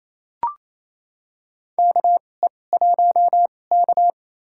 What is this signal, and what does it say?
Morse code, heard as a keyed tone, about 28 wpm, spaced as for 20 wpm, 700 Hz. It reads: KE1K
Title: USA Call Signs - 4 Characters - 2x Character Spacing - 28wpm